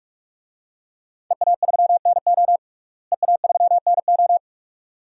Morse code, heard as a keyed tone, about 43 wpm, 700 Hz. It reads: EA3NO EA3NO
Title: DXCC-100 Call Signs - 5 Characters - Repeated 2x - 45wpm